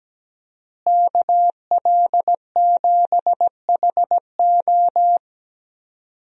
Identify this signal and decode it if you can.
Morse code, heard as a keyed tone, about 17 wpm, 700 Hz. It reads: KL7HO
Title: DXCC-100 Call Signs - 5 Characters - Repeated 2x - 17wpm